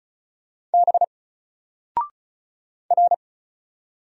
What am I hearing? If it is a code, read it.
Morse code, heard as a keyed tone, about 34 wpm, 700 Hz. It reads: B R
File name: Single-Letter-Number-35wpm